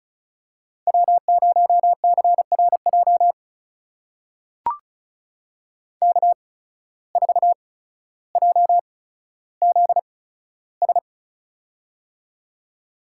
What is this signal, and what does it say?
Morse code, heard as a keyed tone, about 35 wpm, spaced as for 15 wpm, 690 Hz. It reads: W0CRJ K4JZS
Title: USA Call Signs - 5 Characters - 8x Character Spacing - 35wpm